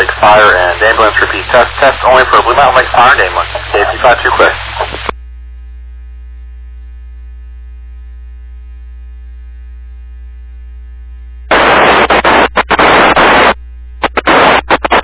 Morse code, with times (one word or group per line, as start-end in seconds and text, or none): none